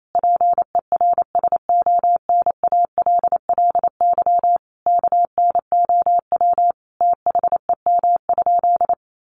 Morse code, heard as a keyed tone, about 28 wpm, 700 Hz. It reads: PERSONALLY KNOW THEM?
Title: Sentences from Top 1400 Words 28wpm